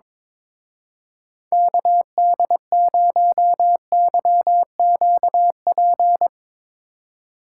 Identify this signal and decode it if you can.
Morse code, heard as a keyed tone, about 22 wpm, 700 Hz. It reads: KD0YQP
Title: Speed Racing - USA 2x3 Call Signs 22wpm